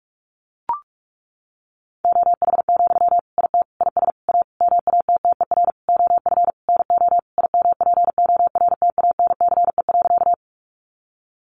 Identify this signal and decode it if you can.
Morse code, heard as a keyed tone, about 44 wpm, 700 Hz. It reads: OH, IT IS A MATTER OF NO IMPORTANCE.
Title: Sentences from Top 1200 Words 45wpm